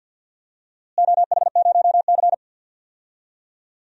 Morse code, heard as a keyed tone, about 47 wpm, 700 Hz. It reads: OH0C
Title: DXCC-100 Call Signs - 4 Characters - Repeated 2x - 50wpm